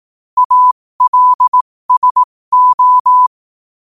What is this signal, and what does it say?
Morse code, heard as a keyed tone, about 18 wpm, spaced as for 15 wpm, 1000 Hz. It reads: ALSO
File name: qotdfiles / 15wpm